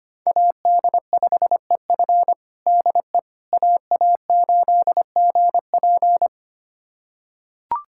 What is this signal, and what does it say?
Morse code, heard as a keyed tone, about 25 wpm, 700 Hz. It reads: AD5EF DE AA8GP
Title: USA Callsign DE Callsign - 5 Characters - 25wpm